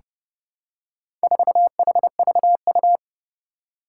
Morse code, heard as a keyed tone, about 30 wpm, 700 Hz. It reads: 4HVU
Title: Four Challenging Letter-Number Combinations 30wpm